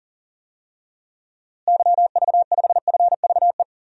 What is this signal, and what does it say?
Morse code, heard as a keyed tone, about 40 wpm, 700 Hz. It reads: YV5FVE
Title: DXCC-100 Call Signs - 6 Characters - Repeated 3x - 40wpm